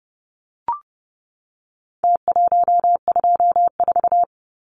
Morse code, heard as a keyed tone, about 30 wpm, 700 Hz. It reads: T124
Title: Numbers - Contest Style - 4-Digit 30wpm